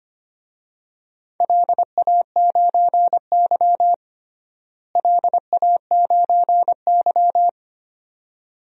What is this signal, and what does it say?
Morse code, heard as a keyed tone, about 25 wpm, 700 Hz. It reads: LA9Y LA9Y
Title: DXCC-100 Call Signs - 4 Characters - Repeated 2x - 25wpm